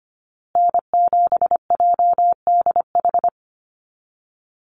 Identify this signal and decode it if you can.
Morse code, heard as a keyed tone, about 25 wpm, 700 Hz. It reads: N7JDH
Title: USA Call Signs Repeated 2x 25wpm